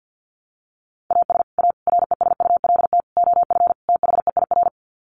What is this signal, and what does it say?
Morse code, heard as a keyed tone, about 50 wpm, 700 Hz. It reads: AS A RESULT OF THEIR
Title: Sets of 5 Words Encore2 50wpm